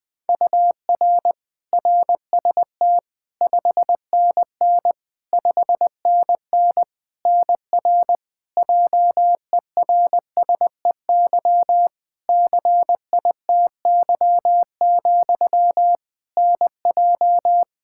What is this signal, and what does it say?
Morse code, heard as a keyed tone, about 21 wpm, 700 Hz. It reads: UR RST 5NN 5NN NR JERSEY CITY, NJ
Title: Practice QSO - Exchange 1 - Beginning - 20wpm